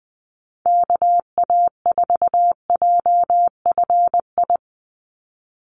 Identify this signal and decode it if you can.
Morse code, heard as a keyed tone, about 20 wpm, 700 Hz. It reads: KA4JFI
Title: USA Call Signs 20wpm